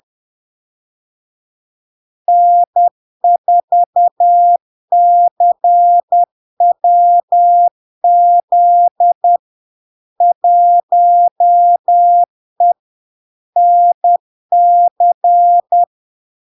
Morse code, heard as a keyed tone, about 10 wpm, 700 Hz. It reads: N4CWZ 1E NC
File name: LARK-FD-2022-Log-in-CW-10wpm-700-Hz